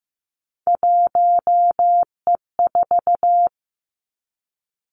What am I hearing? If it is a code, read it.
Morse code, heard as a keyed tone, about 15 wpm, 700 Hz. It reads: 1E4
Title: Three Challenging Letter-Number Combinations 15wpm